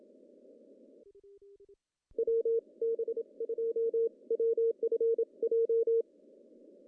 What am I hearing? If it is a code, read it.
Morse code, heard as a keyed tone, about 27 wpm, spaced as for 23 wpm, 460 Hz. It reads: WB2WFJ
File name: CW / cqww-cw-k5tr